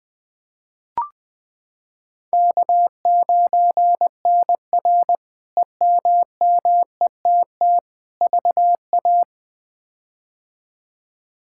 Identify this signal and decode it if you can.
Morse code, heard as a keyed tone, about 20 wpm, 700 Hz. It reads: K9NR EMMETT VA
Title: K1USN SST QSO Practice - 20wpm